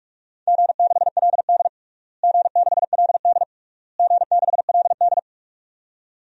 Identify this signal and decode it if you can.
Morse code, heard as a keyed tone, about 44 wpm, 700 Hz. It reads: G6LD G6LD G6LD
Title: DXCC-100 Call Signs - 4 Characters - Repeated 3x - 45wpm